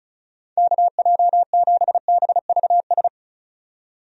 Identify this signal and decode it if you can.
Morse code, heard as a keyed tone, about 34 wpm, 700 Hz. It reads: KJ7BVS
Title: USA Call Signs - 6 Characters - 35wpm